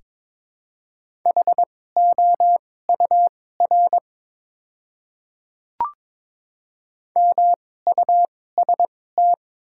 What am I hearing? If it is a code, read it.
Morse code, heard as a keyed tone, about 22 wpm, spaced as for 16 wpm, 690 Hz. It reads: HOUR MUST
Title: 2x Character Spacing - Top 500 Words - 22wpm